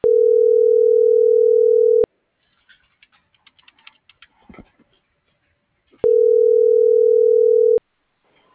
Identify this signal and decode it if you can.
Ringback tone (North America), 440 + 480 Hz.
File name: CENTRAL BLN / out ericka.canales - .51265